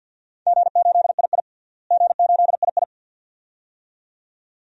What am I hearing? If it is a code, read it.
Morse code, heard as a keyed tone, about 51 wpm, 700 Hz. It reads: G8II G8II
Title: DXCC-100 Call Signs - 4 Characters - Repeated 2x - 50wpm